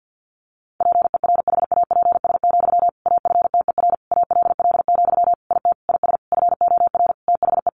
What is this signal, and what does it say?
Morse code, heard as a keyed tone, about 50 wpm, 700 Hz. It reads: PERHAPS, AFTER ALL, IT IS FOR THE
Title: Sentences from Top 500 Words 50wpm